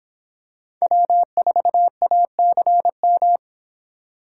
Morse code, heard as a keyed tone, about 24 wpm, 700 Hz. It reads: W4ACM